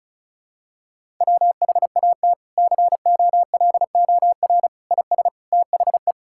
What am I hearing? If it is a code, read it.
Morse code, heard as a keyed tone, about 36 wpm, 700 Hz. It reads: WHAT COLOR IS THE
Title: Sentences from Top 1000 Words - Review Part 6 - 35wpm